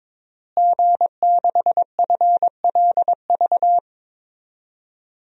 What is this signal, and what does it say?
Morse code, heard as a keyed tone, about 21 wpm, 700 Hz. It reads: G6FLV
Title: DXCC-100 Call Signs - 5 Characters - Repeated 3x - 22wpm